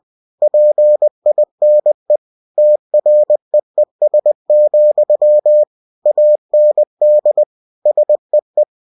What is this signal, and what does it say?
Morse code, heard as a keyed tone, about 20 wpm, 600 Hz. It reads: PINE TREES, AND SEE
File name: cw-books / winnie-the-pooh / winnie-20wpm-page-0038